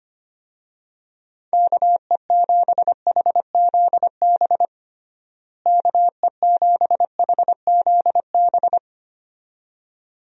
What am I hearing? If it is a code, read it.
Morse code, heard as a keyed tone, about 25 wpm, 700 Hz. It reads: KE7HZB KE7HZB
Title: USA Call Signs - 6 Characters - Repeated 2x - 25wpm